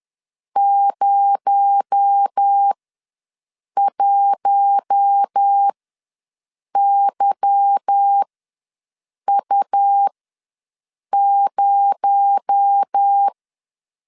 Morse code, heard as a keyed tone, about 10 wpm, spaced as for 6 wpm, 800 Hz. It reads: 01YU0
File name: LEZIONI / 9 28 ws 10wpm 4124